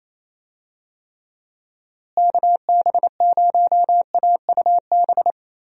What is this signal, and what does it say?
Morse code, heard as a keyed tone, about 28 wpm, 700 Hz. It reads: KB0AUB
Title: USA Call Signs - 6 Characters - Repeated 3x - 28wpm